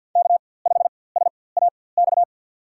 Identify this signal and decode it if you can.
Morse code, heard as a keyed tone, about 47 wpm, spaced as for 23 wpm, 700 Hz. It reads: K5SAX
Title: USA Call Signs - 5 Characters - 4x Character Spacing - 50wpm